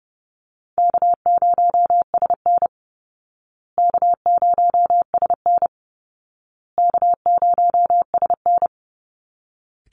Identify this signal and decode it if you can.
Morse code, heard as a keyed tone, about 30 wpm, 700 Hz. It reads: K0SN K0SN K0SN
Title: Call Signs Repeated 2x - 30wpm